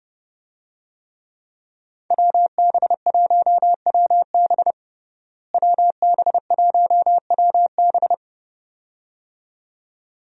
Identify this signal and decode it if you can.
Morse code, heard as a keyed tone, about 30 wpm, 700 Hz. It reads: WB1WB WB1WB
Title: USA Call Signs - 5 Characters - Repeated 2x - 30wpm